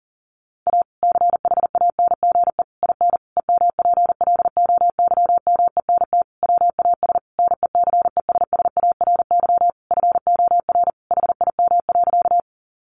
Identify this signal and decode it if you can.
Morse code, heard as a keyed tone, about 40 wpm, 700 Hz. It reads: A CHANGE IN EMPLOYMENT WAS NECESSARY FOR HIM.
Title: Sentences from Top 1500 Words 40wpm